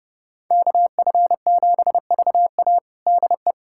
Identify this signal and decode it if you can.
Morse code, heard as a keyed tone, about 31 wpm, 700 Hz. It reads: KF7VA DE
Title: USA Callsign DE Callsign - 5 Characters - 30wpm